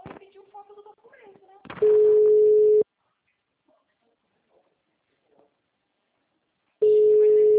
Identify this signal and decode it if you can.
Ringback tone (Europe, 425 Hz), with speech.
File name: out- -2002- - - .19292